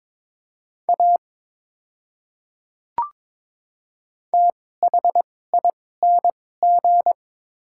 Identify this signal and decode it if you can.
Morse code, heard as a keyed tone, about 22 wpm, spaced as for 16 wpm, 700 Hz. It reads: A THING
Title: 2x Character Spacing - Top 100 Words - 22wpm